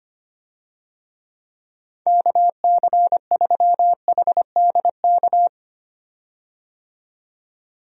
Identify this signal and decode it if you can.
Morse code, heard as a keyed tone, about 25 wpm, 700 Hz. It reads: KC3HDK
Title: USA Call Signs - 6 Characters - 25wpm